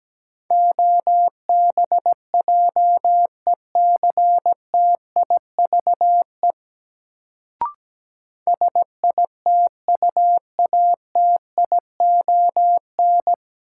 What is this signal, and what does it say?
Morse code, heard as a keyed tone, about 17 wpm, 700 Hz. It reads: OBJECTIVE SITUATION
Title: Sentences from Top 1200 Words - New Word Drill - 17wpm